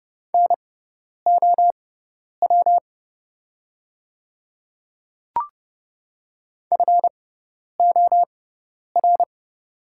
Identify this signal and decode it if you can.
Morse code, heard as a keyed tone, about 30 wpm, spaced as for 10 wpm, 700 Hz. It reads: NOW FOR
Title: 6x Character Spacing - Top 100 Words - 30wpm